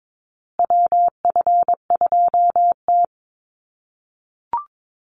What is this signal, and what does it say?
Morse code, heard as a keyed tone, about 22 wpm, 700 Hz. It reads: WF2T
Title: USA Call Signs Repeated 2x 22wpm